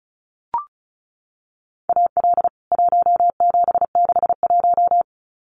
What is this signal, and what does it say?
Morse code, heard as a keyed tone, about 34 wpm, 700 Hz. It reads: AL 1761
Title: CWT Exchange 35wpm